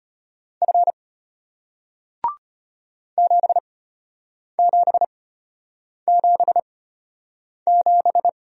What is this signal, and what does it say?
Morse code, heard as a keyed tone, about 34 wpm, spaced as for 8 wpm, 700 Hz. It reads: F 7777